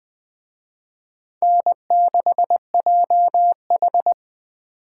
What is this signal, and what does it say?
Morse code, heard as a keyed tone, about 20 wpm, 700 Hz. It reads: N6JH
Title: Speed Racing - USA 1x2 Call Signs 17wpm